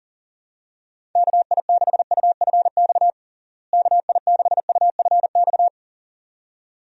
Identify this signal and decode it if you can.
Morse code, heard as a keyed tone, about 37 wpm, 700 Hz. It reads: KI6UFX KI6UFX
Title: USA Call Signs - 6 Characters - Repeated 2x - 40wpm